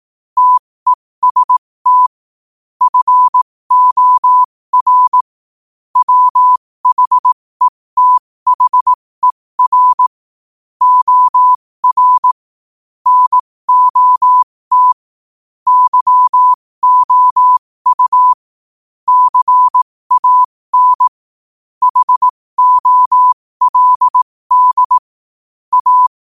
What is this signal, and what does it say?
Morse code, heard as a keyed tone, about 18 wpm, spaced as for 15 wpm, 1000 Hz. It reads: TEST FOR WHETHER OR NOT YOU CAN HOLD A
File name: qotdfiles / 15wpm